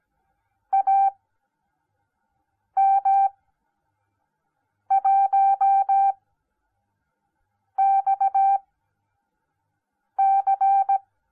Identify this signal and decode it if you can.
Morse code, heard as a keyed tone, about 17 wpm, spaced as for 5 wpm, 780 Hz. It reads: AM1XC